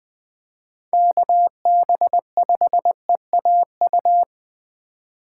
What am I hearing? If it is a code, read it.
Morse code, heard as a keyed tone, about 20 wpm, 700 Hz. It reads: KB5EAU
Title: USA Call Signs - 6 Characters - 8x Character Spacing - 20wpm